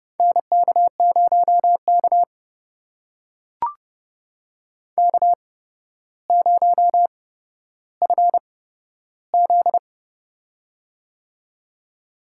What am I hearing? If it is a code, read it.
Morse code, heard as a keyed tone, about 30 wpm, spaced as for 13 wpm, 700 Hz. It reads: NK0K K0FZ
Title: USA Call Signs - 8x Character Spacing - 30wpm